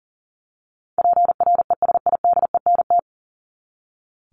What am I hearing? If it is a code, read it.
Morse code, heard as a keyed tone, about 40 wpm, 700 Hz. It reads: PRESIDENT
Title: Top 751-1000 Words 40wpm